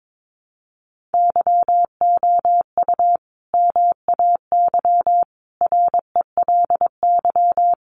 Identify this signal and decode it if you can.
Morse code, heard as a keyed tone, about 22 wpm, 700 Hz. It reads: YOU MAY RELY
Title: Sentences from Top 1200 Words 22wpm